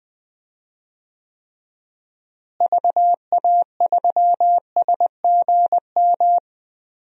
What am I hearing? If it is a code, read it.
Morse code, heard as a keyed tone, about 20 wpm, 700 Hz. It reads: VA3SGM
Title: DXCC-100 Call Signs - 6 Characters - Repeated 2x - 20wpm